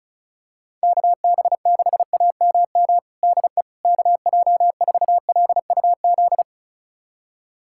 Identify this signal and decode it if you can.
Morse code, heard as a keyed tone, about 34 wpm, 700 Hz. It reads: KB6AMM DE KJ4LUZ
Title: USA Callsign DE Callsign - 6 Characters - 35wpm